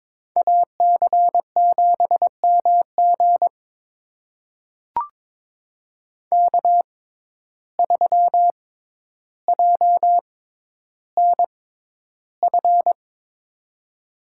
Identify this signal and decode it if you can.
Morse code, heard as a keyed tone, about 22 wpm, spaced as for 11 wpm, 700 Hz. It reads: AC7MG K3JNF
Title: USA Call Signs - 5 Characters - 6x Character Spacing - 22wpm